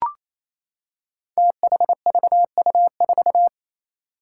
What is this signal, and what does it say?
Morse code, heard as a keyed tone, about 28 wpm, 700 Hz. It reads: THVU4
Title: Five Challenging Letter-Number Combinations 28wpm